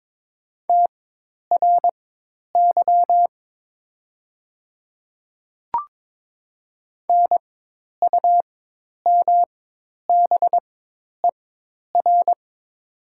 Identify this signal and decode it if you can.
Morse code, heard as a keyed tone, about 22 wpm, spaced as for 10 wpm, 700 Hz. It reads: TRY NUMBER
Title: 4x Character Spacing - Top 200 Words - 22wpm